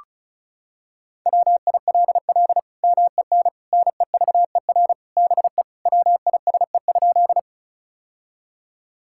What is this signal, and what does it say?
Morse code, heard as a keyed tone, about 36 wpm, 700 Hz. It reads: WILL MEN NEVER BE WISE?
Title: Sentences from Top 1600 Words 35wpm